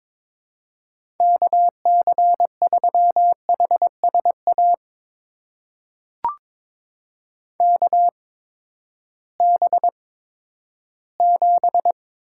Morse code, heard as a keyed tone, about 22 wpm, spaced as for 13 wpm, 700 Hz. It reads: KC3HSA KB7
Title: USA Call Signs - 6 Characters - 8x Character Spacing - 22wpm